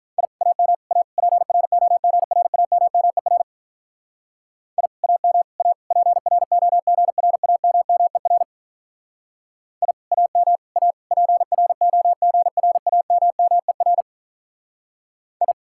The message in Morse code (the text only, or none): I AM A PROGRAMMER I AM A PROGRAMMER I AM A PROGRAMMER I